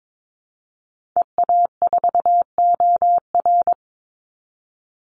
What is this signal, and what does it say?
Morse code, heard as a keyed tone, about 22 wpm, 700 Hz. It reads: EA4OR
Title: Call Signs Repeated 2x - 22wpm